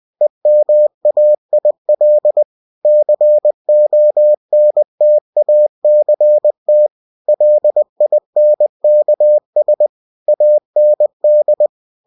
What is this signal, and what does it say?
Morse code, heard as a keyed tone, about 20 wpm, 600 Hz. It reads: EMAIL CONTACT LINKS AND
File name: cw-books / winnie-the-pooh / winnie-20wpm-page-0121